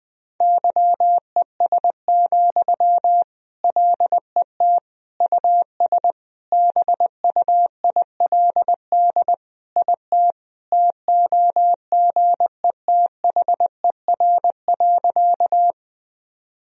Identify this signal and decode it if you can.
Morse code, heard as a keyed tone, about 20 wpm, 700 Hz. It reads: YES, LET US BUILD IT TOGETHER.
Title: Sentences from Top 1000 Words - Review Part 1 - 20wpm